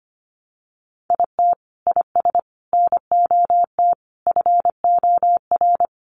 Morse code, heard as a keyed tone, about 25 wpm, 700 Hz. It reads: IT IS NOT FOR
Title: Sentences from Top 600 Words 25wpm